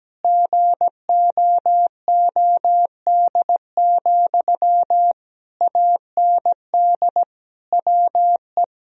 Morse code, heard as a keyed tone, about 17 wpm, 700 Hz. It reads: GOOD, AND WE
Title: Sentences from Top 1000 Words - Review Part 1 - 17wpm